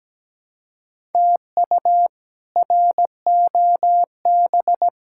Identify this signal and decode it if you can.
Morse code, heard as a keyed tone, about 17 wpm, 700 Hz. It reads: TU ROB
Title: K1USN SST QSO Practice - 17wpm